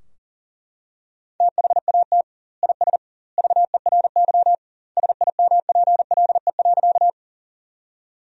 Morse code, heard as a keyed tone, about 40 wpm, 700 Hz. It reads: THAT IS VERY SIMPLE.
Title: 2x Word-Spacing - Sentences from Top 400 Words - 40wpm